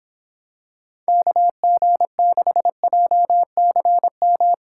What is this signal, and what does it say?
Morse code, heard as a keyed tone, about 26 wpm, 700 Hz. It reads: KG6JCM